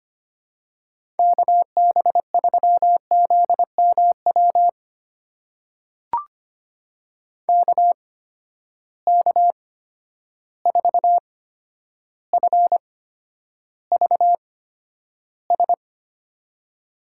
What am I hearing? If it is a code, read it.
Morse code, heard as a keyed tone, about 25 wpm, spaced as for 11 wpm, 700 Hz. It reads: KB3ZMW KK4FVS
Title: USA Call Signs - 6 Characters - 8x Character Spacing - 25wpm